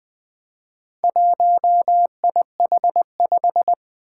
Morse code, heard as a keyed tone, about 20 wpm, 700 Hz. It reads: 1IH5
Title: Four Challenging Letter-Number Combinations 20wpm